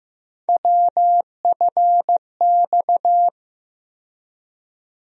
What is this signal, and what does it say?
Morse code, heard as a keyed tone, about 15 wpm, 700 Hz. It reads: WFX
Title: USA Call Sign Postfixes - 3 Letters 15wpm